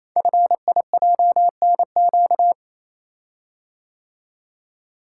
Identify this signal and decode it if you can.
Morse code, heard as a keyed tone, about 28 wpm, 700 Hz. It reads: FIJNQ
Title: Code Groups - 5 Letters - 28wpm